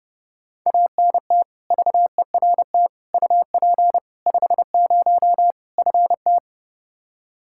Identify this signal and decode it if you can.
Morse code, heard as a keyed tone, about 32 wpm, 700 Hz. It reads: ANT VERT UP 50 FT
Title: Practice QSO - Exchange 2 - Info - 30wpm